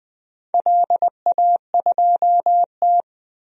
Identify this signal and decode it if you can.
Morse code, heard as a keyed tone, about 20 wpm, 700 Hz. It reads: LA2T
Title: DXCC-100 Call Signs - 4 Characters - 20wpm